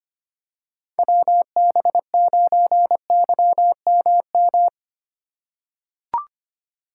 Morse code, heard as a keyed tone, about 25 wpm, 700 Hz. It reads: WB9YMM
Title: USA Call Signs - 6 Characters - 4x Character Spacing - 25wpm